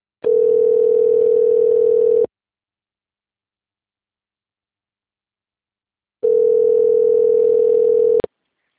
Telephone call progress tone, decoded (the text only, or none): ringback tone